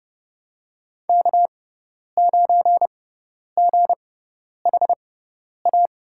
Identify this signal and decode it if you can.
Morse code, heard as a keyed tone, about 32 wpm, spaced as for 10 wpm, 700 Hz. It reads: K9GHA
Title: USA Call Signs - 5 Characters - 6x Character Spacing - 30wpm